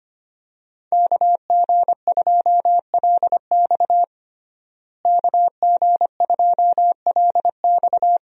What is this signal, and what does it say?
Morse code, heard as a keyed tone, about 25 wpm, 700 Hz. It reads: KG2LX KG2LX
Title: USA Call Signs - 5 Characters - Repeated 3x - 25wpm